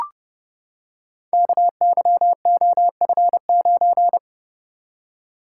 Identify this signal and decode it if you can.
Morse code, heard as a keyed tone, about 30 wpm, 700 Hz. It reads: KYOF9
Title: Code Groups - 5 Letter:Numbers - 30wpm